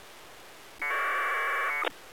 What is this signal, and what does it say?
Packet radio (AX.25, as in APRS).